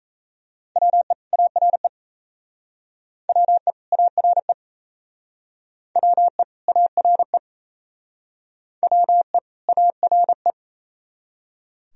Morse code, heard as a keyed tone, about 34 wpm, 700 Hz. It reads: WE ARE WE ARE WE ARE WE ARE
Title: Speed-Racing - Sets of 2 Words 28wpm